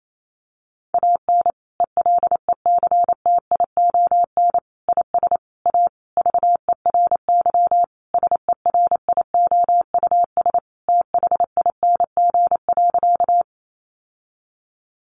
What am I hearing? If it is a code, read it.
Morse code, heard as a keyed tone, about 28 wpm, 700 Hz. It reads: AN ELECTION IS A VERY SERIOUS THING.
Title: Sentences from Top 1500 Words 28wpm